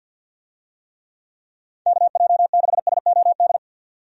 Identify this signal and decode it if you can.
Morse code, heard as a keyed tone, about 47 wpm, 700 Hz. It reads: KJ6SOD
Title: USA Call Signs - 6 Characters - 50wpm